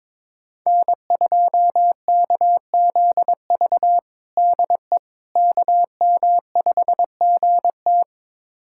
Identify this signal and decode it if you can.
Morse code, heard as a keyed tone, about 22 wpm, 700 Hz. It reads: N2KZV DE KM5GT
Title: USA Callsign DE Callsign - 5 Characters - 22wpm